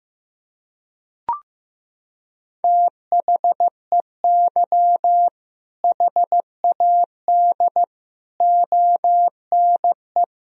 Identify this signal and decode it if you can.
Morse code, heard as a keyed tone, about 15 wpm, 700 Hz. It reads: THEY HAD ONE
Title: Sentences from Top 1000 Words - Review Part 3 - 15wpm